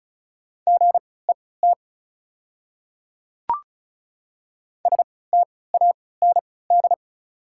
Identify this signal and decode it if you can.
Morse code, heard as a keyed tone, about 34 wpm, spaced as for 20 wpm, 690 Hz. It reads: GET STAND
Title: 3x Character Spacing - Top 300 Words - 35wpm